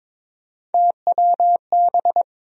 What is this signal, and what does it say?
Morse code, heard as a keyed tone, about 22 wpm, 700 Hz. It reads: TWB
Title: USA Call Sign Postfixes - 3 Letters 22wpm